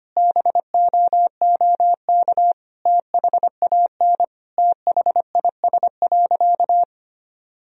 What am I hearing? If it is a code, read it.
Morse code, heard as a keyed tone, about 25 wpm, 700 Hz. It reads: BOOK THAN THIS.
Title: Sentences from Top 1000 Words - Review Part 3 - 25wpm